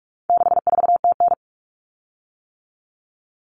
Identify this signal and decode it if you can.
Morse code, heard as a keyed tone, about 44 wpm, 700 Hz. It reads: 64TN
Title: Common Cut Numbers - 4-Digit 45wpm